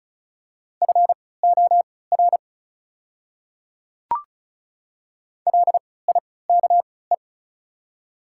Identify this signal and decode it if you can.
Morse code, heard as a keyed tone, about 36 wpm, spaced as for 20 wpm, 690 Hz. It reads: FOR LIKE
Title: 3x Character Spacing - Top 100 Words - 35wpm